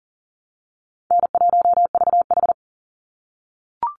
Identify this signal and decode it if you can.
Morse code, heard as a keyed tone, about 40 wpm, 700 Hz. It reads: N1VH
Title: Easy Call Signs - 40wpm